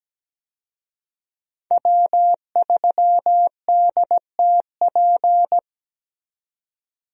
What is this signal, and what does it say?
Morse code, heard as a keyed tone, about 17 wpm, 700 Hz. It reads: W3DTP
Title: USA Call Signs - 5 Characters - 17wpm